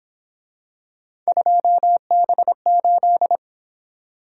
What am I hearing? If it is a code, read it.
Morse code, heard as a keyed tone, about 26 wpm, 700 Hz. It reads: 2B8
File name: Speed-Racing - Three Letter-Number Combinations - 22wpm